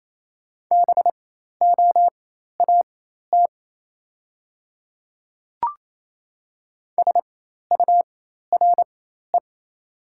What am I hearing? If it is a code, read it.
Morse code, heard as a keyed tone, about 28 wpm, spaced as for 13 wpm, 700 Hz. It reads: BOAT SURE
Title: 4x Character Spacing - Top 500 Words - 28wpm